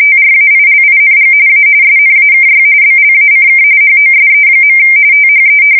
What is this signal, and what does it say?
RTTY, about 45.45 baud. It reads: TRAILING A SNAKY BLACK HOSE. THE
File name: fsk170 / USS-New-Jersey